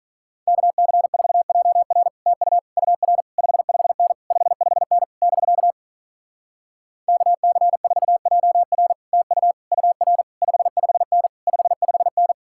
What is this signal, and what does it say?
Morse code, heard as a keyed tone, about 46 wpm, 700 Hz. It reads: KC4JR TU UR 55N 55N <BK> KC4JR TU UR 55N 55N